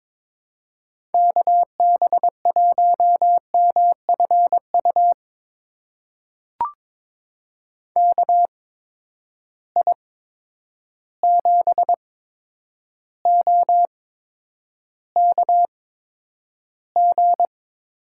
Morse code, heard as a keyed tone, about 22 wpm, spaced as for 9 wpm, 700 Hz. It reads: KB1MFU KI7OKG
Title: USA Call Signs - 6 Characters - 8x Character Spacing - 22wpm